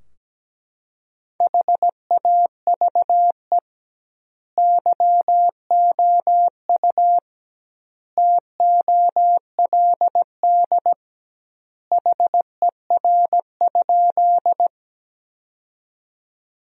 Morse code, heard as a keyed tone, about 17 wpm, 700 Hz. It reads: HAVE YOU TOLD HER?
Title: 2x Word-Spacing - Sentences from Top 400 Words - 17wpm